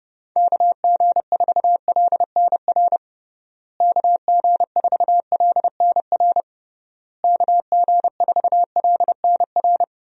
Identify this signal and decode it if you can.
Morse code, heard as a keyed tone, about 30 wpm, 700 Hz. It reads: KG4LNR KG4LNR KG4LNR
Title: USA Call Signs - 6 Characters - Repeated 3x - 30wpm